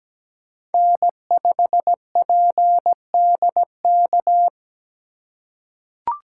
Morse code, heard as a keyed tone, about 17 wpm, 700 Hz. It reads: N5PDK
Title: USA Call Signs - 5 Characters - 2x Character Spacing - 17wpm